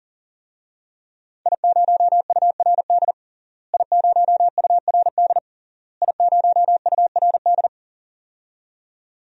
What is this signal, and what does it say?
Morse code, heard as a keyed tone, about 40 wpm, 700 Hz. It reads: I0URD I0URD I0URD
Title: DXCC-100 Call Signs - 5 Characters - Repeated 3x - 40wpm